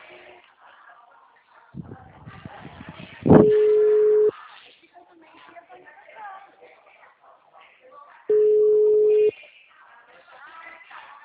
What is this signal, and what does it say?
Ringback tone (Europe, 425 Hz), with speech.
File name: out- -2003- - - .39671